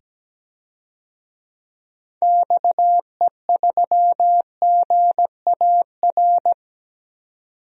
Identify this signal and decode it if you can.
Morse code, heard as a keyed tone, about 17 wpm, 700 Hz. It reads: XE3GAR
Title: DXCC-100 Call Signs - 6 Characters - 17wpm